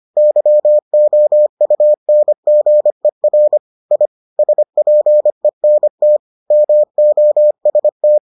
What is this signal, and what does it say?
Morse code, heard as a keyed tone, about 25 wpm, 600 Hz. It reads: YOUNGER I SPENT MOST